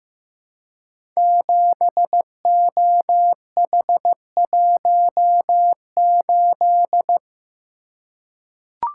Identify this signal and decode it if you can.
Morse code, heard as a keyed tone, about 15 wpm, 700 Hz. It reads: 7OH18
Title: Code Groups - 5 Letter:Numbers - 15wpm